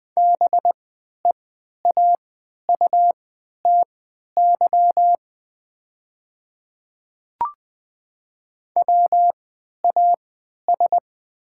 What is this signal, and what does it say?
Morse code, heard as a keyed tone, about 20 wpm, spaced as for 11 wpm, 690 Hz. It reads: BEAUTY WAS
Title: 3x Character Spacing - Top 500 Words - 20wpm